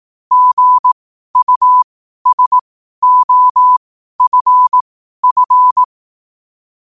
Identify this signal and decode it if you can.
Morse code, heard as a keyed tone, about 18 wpm, spaced as for 12 wpm, 1000 Hz. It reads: GUSOFF